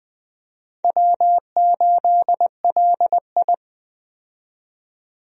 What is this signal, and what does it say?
Morse code, heard as a keyed tone, about 20 wpm, 700 Hz. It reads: W8LI
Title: USA Call Signs - 4 Characters - 20wpm